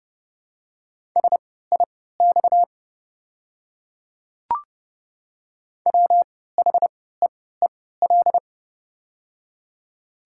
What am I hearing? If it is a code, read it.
Morse code, heard as a keyed tone, about 29 wpm, spaced as for 17 wpm, 700 Hz. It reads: SIX WHEEL
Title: 3x Character Spacing - Top 500 Words - 30wpm